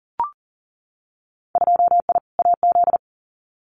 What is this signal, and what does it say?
Morse code, heard as a keyed tone, about 40 wpm, 700 Hz. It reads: 2I AZ
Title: Winter Field Day Exchanges 40wpm